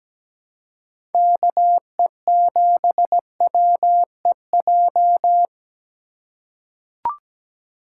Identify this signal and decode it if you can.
Morse code, heard as a keyed tone, about 17 wpm, 700 Hz. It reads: KE7WEJ
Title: USA Call Signs - 6 Characters - 4x Character Spacing - 17wpm